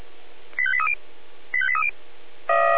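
Five-tone selective call (ZVEI), heard as quick runs of five tones, tones of about 70 ms. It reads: 85830 85830
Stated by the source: ZVEI five-tone call 85855 received by SDR.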